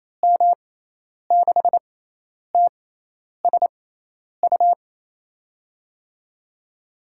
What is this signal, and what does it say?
Morse code, heard as a keyed tone, about 28 wpm, spaced as for 10 wpm, 700 Hz. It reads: M6TSU
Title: USA Call Signs - 6 Characters - 6x Character Spacing - 28wpm